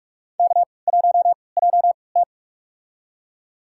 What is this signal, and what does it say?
Morse code, heard as a keyed tone, about 46 wpm, spaced as for 26 wpm, 700 Hz. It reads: K1JT
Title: USA Call Signs - 4 Characters - 3x Character Spacing - 45wpm